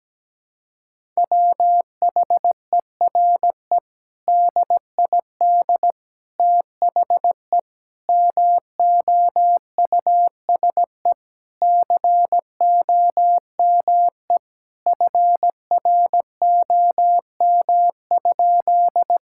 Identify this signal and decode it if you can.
Morse code, heard as a keyed tone, about 17 wpm, 700 Hz. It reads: WHERE DID THE MOUSE COME FROM?
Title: Sentences from Top 1000 Words - Review Part 5 - 17wpm